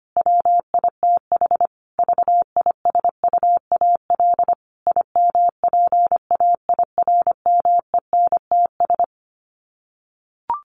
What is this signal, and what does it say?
Morse code, heard as a keyed tone, about 25 wpm, 700 Hz. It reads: WITH VISUAL IMPAIRMENTS
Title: Sets of 5 Words Encore 25wpm